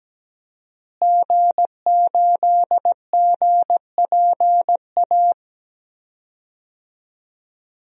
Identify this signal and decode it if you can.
Morse code, heard as a keyed tone, about 17 wpm, 700 Hz. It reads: G8GPA
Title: DXCC-100 Call Signs - 5 Characters - Repeated 2x - 17wpm